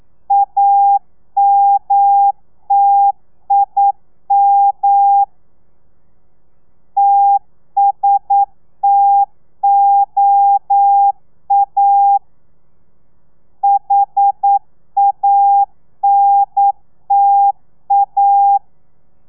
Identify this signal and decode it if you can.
Morse code, heard as a keyed tone, about 9 wpm, 800 Hz. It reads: AMTIM TSTOA HANTA
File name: telegrafia / i2tfj / Lez2 40